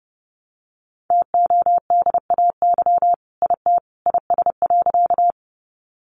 Morse code, heard as a keyed tone, about 30 wpm, 700 Hz. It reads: TODAY IT IS.
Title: Sentences from Top 600 Words 30wpm